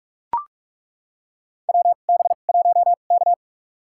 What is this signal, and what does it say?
Morse code, heard as a keyed tone, about 42 wpm, spaced as for 32 wpm, 700 Hz. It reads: WB1K
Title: USA Call Signs - 2x Character Spacing - 45wpm